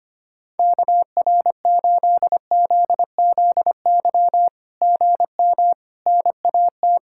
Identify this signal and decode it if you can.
Morse code, heard as a keyed tone, about 24 wpm, 700 Hz. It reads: KR8ZZY GM NAT
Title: SOTA QSO Practice 25wpm